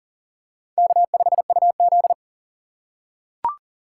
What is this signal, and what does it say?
Morse code, heard as a keyed tone, about 40 wpm, 700 Hz. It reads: K5UZ
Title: USA Call Signs - 4 Characters - 4x Character Spacing - 40wpm